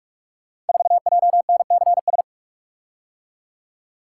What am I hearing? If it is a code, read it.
Morse code, heard as a keyed tone, about 44 wpm, 700 Hz. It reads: 4JNCS
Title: Code Groups - 5 Letter:Numbers - 45wpm